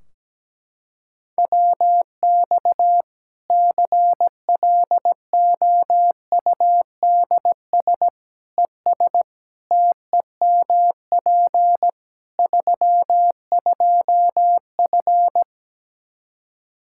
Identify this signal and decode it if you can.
Morse code, heard as a keyed tone, about 17 wpm, 700 Hz. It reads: WX CLOUDS ES TEMP 32F
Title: Practice QSO - Exchange 2 - Info - 17wpm